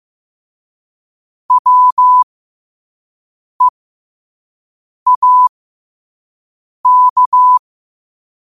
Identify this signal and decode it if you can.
Morse code, heard as a keyed tone, about 15 wpm, spaced as for 5 wpm, 1000 Hz. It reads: WEAK